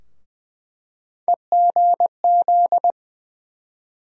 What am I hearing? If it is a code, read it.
Morse code, heard as a keyed tone, about 21 wpm, 700 Hz. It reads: EGZ
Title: USA Call Sign Postfixes - 3 Letters 20wpm